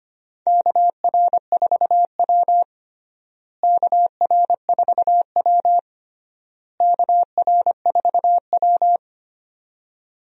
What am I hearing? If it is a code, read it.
Morse code, heard as a keyed tone, about 25 wpm, 700 Hz. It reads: KR4W KR4W KR4W
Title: USA Call Signs - 4 Characters - Repeated 3x - 25wpm